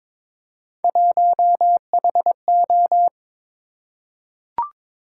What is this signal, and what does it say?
Morse code, heard as a keyed tone, about 22 wpm, 700 Hz. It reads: 1HO
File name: practice / 22wpm / Three Letter-Number Combinations - 22wpm